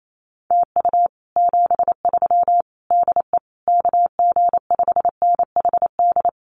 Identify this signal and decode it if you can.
Morse code, heard as a keyed tone, about 28 wpm, 700 Hz. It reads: TU 73 DE KG5NHD
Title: Common QSO Phrases Encore 28wpm